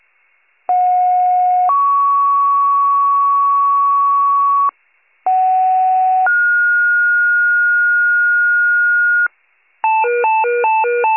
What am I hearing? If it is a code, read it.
Two-tone page, heard as a one-second tone, then a three-second tone, about 730 then 1120 Hz, about 750 then 1430 Hz, with noise. It is heard from 0.7 to 9.3 s.